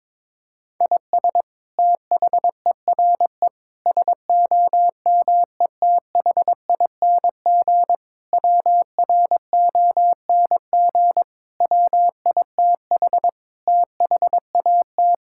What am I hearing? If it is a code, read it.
Morse code, heard as a keyed tone, about 22 wpm, 700 Hz. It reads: IS THERE SOMETHING WRONG WITH THAT
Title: Sentences from Top 1000 Words - Review Part 6 - 22wpm